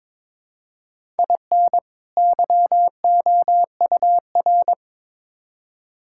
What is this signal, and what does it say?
Morse code, heard as a keyed tone, about 21 wpm, 700 Hz. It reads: IN YOUR
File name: Build A Phrase - 2-Word Phrases - 22wpm